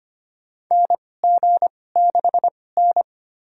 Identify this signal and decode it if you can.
Morse code, heard as a keyed tone, about 25 wpm, spaced as for 18 wpm, 700 Hz. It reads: NG6N
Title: USA Call Signs - 4 Characters - 2x Character Spacing - 25wpm